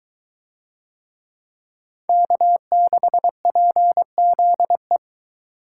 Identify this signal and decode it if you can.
Morse code, heard as a keyed tone, about 22 wpm, 700 Hz. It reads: K6PZE